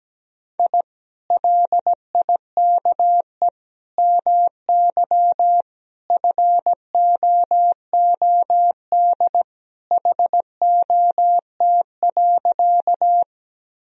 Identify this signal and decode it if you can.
Morse code, heard as a keyed tone, about 17 wpm, 700 Hz. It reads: I LIKE MY FOOD HOT.
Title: Sentences from Top 1000 Words - Review Part 3 - 17wpm